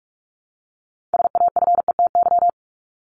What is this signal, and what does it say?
Morse code, heard as a keyed tone, about 44 wpm, 700 Hz. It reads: SAFETY
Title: Top 501-750 Words 45wpm